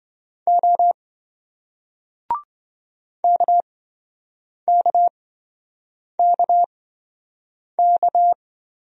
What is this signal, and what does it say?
Morse code, heard as a keyed tone, about 27 wpm, spaced as for 7 wpm, 700 Hz. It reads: O KKKK